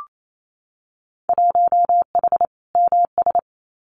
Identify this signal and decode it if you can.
Morse code, heard as a keyed tone, about 28 wpm, 700 Hz. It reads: 1H MS
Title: Winter Field Day Exchanges 28wpm